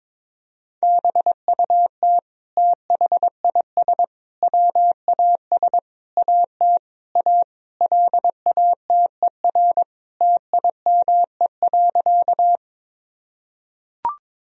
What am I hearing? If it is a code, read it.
Morse code, heard as a keyed tone, about 21 wpm, 700 Hz. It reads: BUT THIS WAS AT A LATER TIME.
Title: Sentences from Top 1000 Words - Review Part 4 - 22wpm